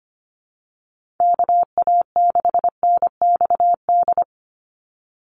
Morse code, heard as a keyed tone, about 25 wpm, 700 Hz. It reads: KA6NXD
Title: USA Call Signs 25wpm